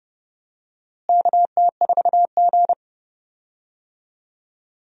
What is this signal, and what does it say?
Morse code, heard as a keyed tone, about 32 wpm, 700 Hz. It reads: KT4G
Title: USA Call Signs - 4 Characters - Repeated 3x - 30wpm